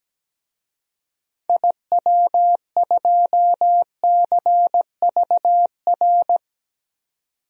Morse code, heard as a keyed tone, about 17 wpm, 700 Hz. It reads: IW2CVR
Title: DXCC-100 Call Signs - 6 Characters - 17wpm